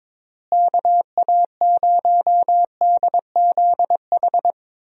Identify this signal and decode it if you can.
Morse code, heard as a keyed tone, about 22 wpm, 700 Hz. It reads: KA0DZH
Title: USA Call Signs - 6 Characters - Repeated 3x - 22wpm